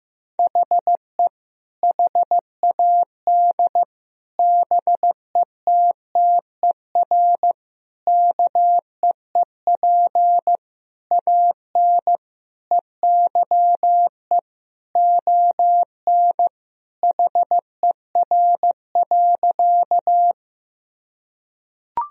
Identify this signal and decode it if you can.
Morse code, heard as a keyed tone, about 15 wpm, 700 Hz. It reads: HE HAD BETTER KEEP AN EYE ON HER.
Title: Sentences from Top 1000 Words - Review Part 1 - 15wpm